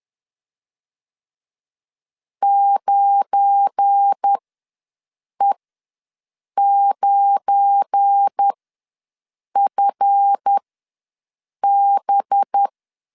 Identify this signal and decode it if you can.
Morse code, heard as a keyed tone, about 10 wpm, spaced as for 6 wpm, 800 Hz. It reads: 9E9FB
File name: LEZIONI / 7 GF9Z / 7 GF9 ws 10wpm 4034